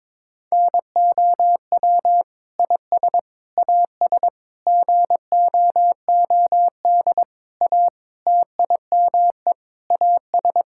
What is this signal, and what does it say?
Morse code, heard as a keyed tone, about 22 wpm, 700 Hz. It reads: NOW IS AS GOOD A TIME AS
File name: practice / 22wpm / Sentences with 8-Word Phrases - 22wpm